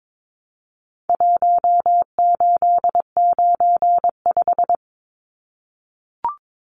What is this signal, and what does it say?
Morse code, heard as a keyed tone, about 22 wpm, 700 Hz. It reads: 1895
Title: Numbers - Contest Style - 4-Digit 22wpm